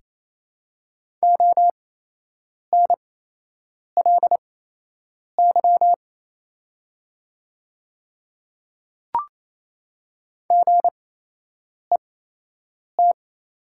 Morse code, heard as a keyed tone, about 28 wpm, spaced as for 8 wpm, 700 Hz. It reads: ONLY GET
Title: 8x Character Spacing - Top 300 Words - 28wpm